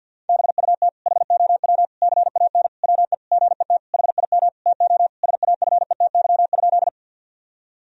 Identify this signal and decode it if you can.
Morse code, heard as a keyed tone, about 50 wpm, 700 Hz. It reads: BUT HOW CAN WE GET HIM TO SAFETY?
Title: Sentences from Top 1000 Words - Review Part 4 - 50wpm